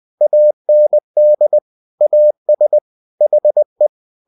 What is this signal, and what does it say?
Morse code, heard as a keyed tone, about 20 wpm, 600 Hz. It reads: AND AS HE
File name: cw-books / winnie-the-pooh / winnie-20wpm-page-0070